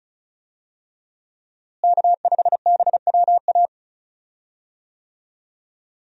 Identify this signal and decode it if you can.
Morse code, heard as a keyed tone, about 36 wpm, 700 Hz. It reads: K5BWA
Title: USA Call Signs - 5 Characters - Repeated 3x - 35wpm